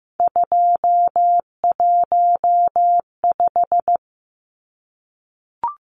Morse code, heard as a keyed tone, about 15 wpm, 700 Hz. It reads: 215
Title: Numbers - 3-Digit 15wpm